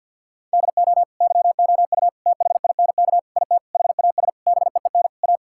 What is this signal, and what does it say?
Morse code, heard as a keyed tone, about 50 wpm, 700 Hz. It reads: DO YOU THINK IT HAS BEEN A
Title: Sentences from Top 1000 Words - Review Part 6 - 50wpm